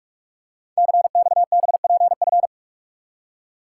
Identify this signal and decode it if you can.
Morse code, heard as a keyed tone, about 44 wpm, 700 Hz. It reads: CXBPF
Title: Code Groups - 5 Letters - 45wpm